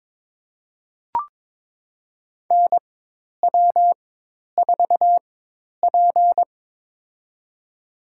Morse code, heard as a keyed tone, about 23 wpm, spaced as for 10 wpm, 700 Hz. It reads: NW4P
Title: USA Call Signs - 4 Characters - 4x Character Spacing - 22wpm